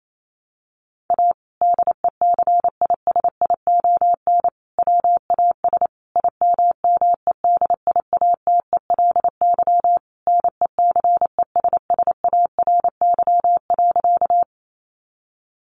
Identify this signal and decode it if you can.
Morse code, heard as a keyed tone, about 28 wpm, 700 Hz. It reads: A DECISION WAS IMMEDIATELY NECESSARY.
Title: Sentences from Top 1100 Words 28wpm